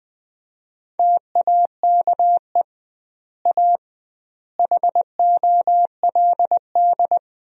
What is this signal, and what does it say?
Morse code, heard as a keyed tone, about 20 wpm, 700 Hz. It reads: TAKE A HOLD
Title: 2x Word-Spacing - Sentences from Top 400 Words - 20wpm